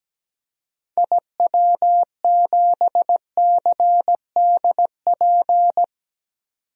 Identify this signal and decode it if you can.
Morse code, heard as a keyed tone, about 17 wpm, 700 Hz. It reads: IW7CDP
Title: DXCC-100 Call Signs - 6 Characters - 17wpm